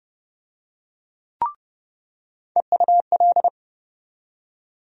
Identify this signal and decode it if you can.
Morse code, heard as a keyed tone, about 29 wpm, 700 Hz. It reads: EUL
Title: USA Call Sign Postfixes - 3 Letters 30wpm